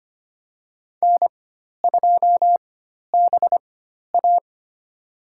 Morse code, heard as a keyed tone, about 25 wpm, spaced as for 12 wpm, 700 Hz. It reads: N2BA
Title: USA Call Signs - 4 Characters - 4x Character Spacing - 25wpm